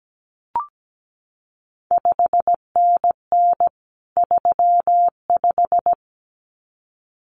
Morse code, heard as a keyed tone, about 17 wpm, 700 Hz. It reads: 5NN 35
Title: CQ WW DX Contest Exchange - 17wpm